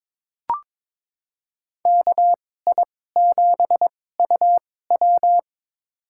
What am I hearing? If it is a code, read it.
Morse code, heard as a keyed tone, about 22 wpm, spaced as for 16 wpm, 700 Hz. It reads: KI7UW
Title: USA Call Signs - 2x Character Spacing - 22wpm